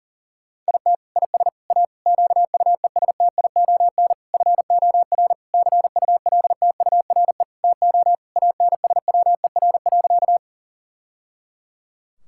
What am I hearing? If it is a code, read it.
Morse code, heard as a keyed tone, about 40 wpm, 690 Hz. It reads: IT IS A QUESTION FOR CULTURE TO ANSWER.
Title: Sentences from Top 1000 Words - Review Part 6 - 40wpm